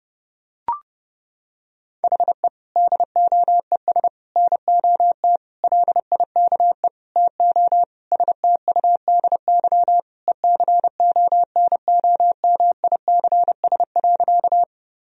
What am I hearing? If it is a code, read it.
Morse code, heard as a keyed tone, about 29 wpm, 700 Hz. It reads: HE DOES NOT LIKE TO STUDY ECONOMICS.
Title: Sentences from Top 1000 Words - Review Part 3 - 30wpm